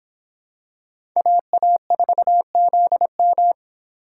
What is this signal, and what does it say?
Morse code, heard as a keyed tone, about 26 wpm, 700 Hz. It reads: AA4ZM